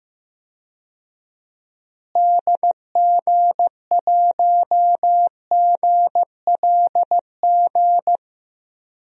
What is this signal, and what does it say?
Morse code, heard as a keyed tone, about 15 wpm, 700 Hz. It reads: DG1GLG
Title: DXCC-100 Call Signs - 6 Characters - Repeated 2x - 15wpm